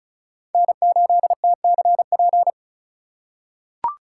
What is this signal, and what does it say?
Morse code, heard as a keyed tone, about 36 wpm, 690 Hz. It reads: N8TCP
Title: USA Call Signs - 5 Characters - 3x Character Spacing - 35wpm